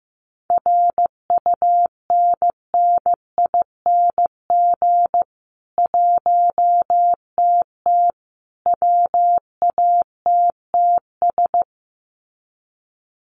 Morse code, heard as a keyed tone, about 15 wpm, 700 Hz. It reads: RUNNING 1TT WATTS
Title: Common QSO Phrases 15wpm